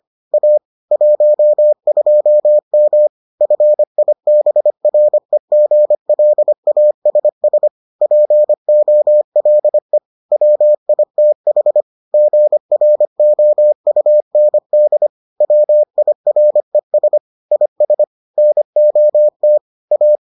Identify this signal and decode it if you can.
Morse code, heard as a keyed tone, about 25 wpm, 600 Hz. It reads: A 12M FIBREGLASS POLE WITH GROUND WIRES IS NOT A